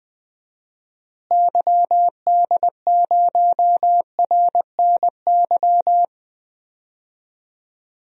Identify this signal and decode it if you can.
Morse code, heard as a keyed tone, about 20 wpm, 700 Hz. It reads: YD0RNY
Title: DXCC-100 Call Signs - 6 Characters - Repeated 3x - 20wpm